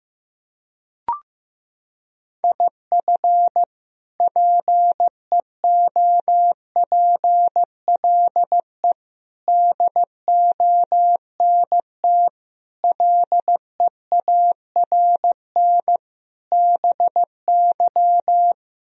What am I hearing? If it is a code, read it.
Morse code, heard as a keyed tone, about 15 wpm, 700 Hz. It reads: IF PEOPLE DONT LEARN BY
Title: Sentences from Top 1000 Words - Review Part 3 - 15wpm